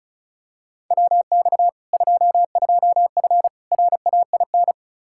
Morse code, heard as a keyed tone, about 36 wpm, 700 Hz. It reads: WX 22F RAIN
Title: Practice QSO - Exchange 2 - Info - 35wpm